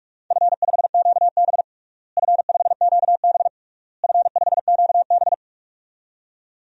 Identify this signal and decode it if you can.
Morse code, heard as a keyed tone, about 46 wpm, 700 Hz. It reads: F5QB F5QB F5QB
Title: DXCC-100 Call Signs - 4 Characters - Repeated 3x - 45wpm